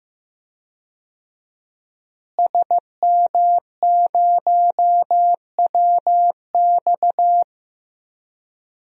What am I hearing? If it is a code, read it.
Morse code, heard as a keyed tone, about 16 wpm, 700 Hz. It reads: SM0WX
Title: DXCC-100 Call Signs - 5 Characters - Repeated 2x - 15wpm